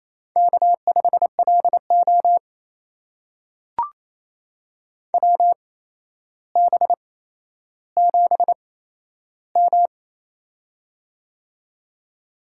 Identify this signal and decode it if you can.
Morse code, heard as a keyed tone, about 28 wpm, spaced as for 12 wpm, 700 Hz. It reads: K5LO WB7M
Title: USA Call Signs - 4 Characters - 8x Character Spacing - 28wpm